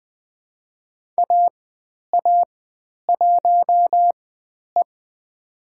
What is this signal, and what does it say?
Morse code, heard as a keyed tone, about 20 wpm, spaced as for 10 wpm, 700 Hz. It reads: AA1E
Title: 102 Lesson 17g - Letter u - Call Signs 20wpm